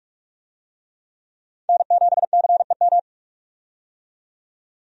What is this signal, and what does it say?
Morse code, heard as a keyed tone, about 42 wpm, 700 Hz. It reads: N7CEM
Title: USA Call Signs - 5 Characters - 45wpm